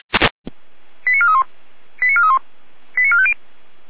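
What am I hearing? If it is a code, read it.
Five-tone selective call (ZVEI), heard as quick runs of five tones, tones of about 70 ms. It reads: 89431 89431 89450
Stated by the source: ZVEI five-tone selective call received on SDR.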